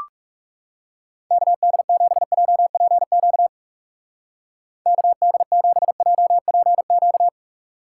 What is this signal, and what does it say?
Morse code, heard as a keyed tone, about 36 wpm, 700 Hz. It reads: KD7JPQ KD7JPQ
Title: Speed Racing - USA 2x3 Call Signs 30wpm